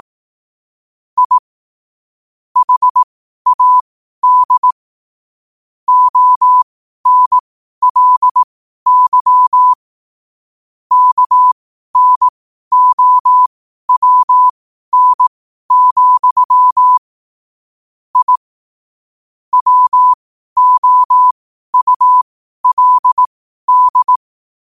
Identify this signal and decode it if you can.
Morse code, heard as a keyed tone, about 18 wpm, spaced as for 12 wpm, 1000 Hz. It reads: I HAD ONLY KNOWN, I WOULD
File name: qotdfiles / 12wpm